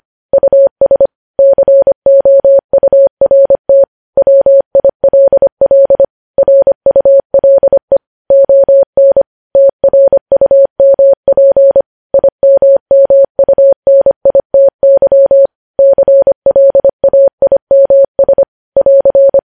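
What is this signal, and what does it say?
Morse code, heard as a keyed tone, about 25 wpm, 550 Hz. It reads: US COURT WILL RULE ON TRUMP IMMUNITY CLAIMS <AR>
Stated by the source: News headlines sent in Morse at 25 wpm.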